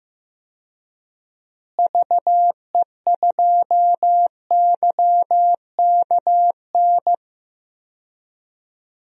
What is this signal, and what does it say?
Morse code, heard as a keyed tone, about 15 wpm, 700 Hz. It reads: VE2YKN
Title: DXCC-100 Call Signs - 6 Characters - Repeated 3x - 15wpm